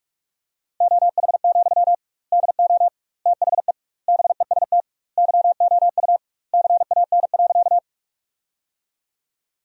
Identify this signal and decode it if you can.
Morse code, heard as a keyed tone, about 41 wpm, 700 Hz. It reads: OH, DO THE BEST YOU CAN.
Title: 2x Word-Spacing - Sentences from Top 500 Words - 45wpm